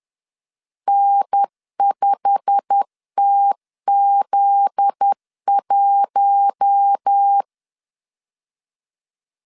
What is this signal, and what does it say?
Morse code, heard as a keyed tone, about 10 wpm, 800 Hz. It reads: N5TZ1
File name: CW / DET / 1 9 1500C S A 10wpm 4511